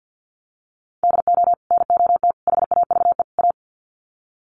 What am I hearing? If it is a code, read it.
Morse code, heard as a keyed tone, about 50 wpm, 700 Hz. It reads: DO NOT HAVE A
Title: Sets of 4 Words 50wpm